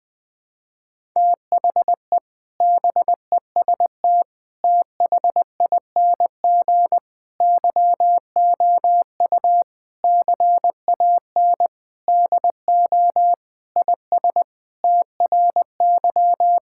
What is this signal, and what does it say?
Morse code, heard as a keyed tone, about 20 wpm, 700 Hz. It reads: THE BEST THING YOU CAN DO IS TRY
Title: Sentences with 7-Word Phrases - Spoken Beginning - 20wpm